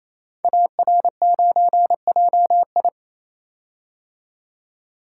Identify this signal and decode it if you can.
Morse code, heard as a keyed tone, about 29 wpm, 700 Hz. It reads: AR9JI
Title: Code Groups - 5 Letter:Numbers - 28wpm